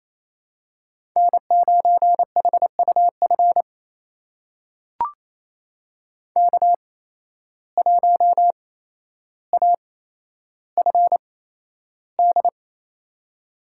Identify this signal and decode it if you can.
Morse code, heard as a keyed tone, about 28 wpm, spaced as for 12 wpm, 700 Hz. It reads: N9HUF K1AFD
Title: USA Call Signs - 5 Characters - 8x Character Spacing - 28wpm